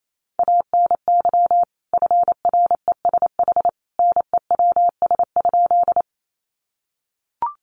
Morse code, heard as a keyed tone, about 28 wpm, 700 Hz. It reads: ANY FRESH NEWS?
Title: Sentences from Top 1200 Words 28wpm